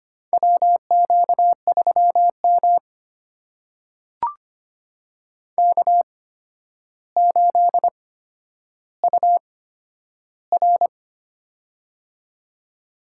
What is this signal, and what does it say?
Morse code, heard as a keyed tone, about 25 wpm, spaced as for 11 wpm, 700 Hz. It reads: WQ3M K8UR
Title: USA Call Signs - 4 Characters - 8x Character Spacing - 25wpm